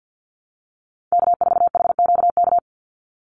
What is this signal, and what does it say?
Morse code, heard as a keyed tone, about 50 wpm, 700 Hz. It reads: K4HQK
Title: Call Signs Repeated 2x - 50wpm